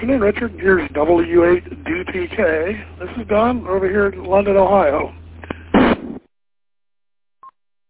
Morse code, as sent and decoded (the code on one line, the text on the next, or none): none